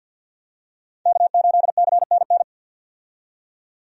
Morse code, heard as a keyed tone, about 50 wpm, 700 Hz. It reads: K8CNN
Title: USA Call Signs - 5 Characters - 50wpm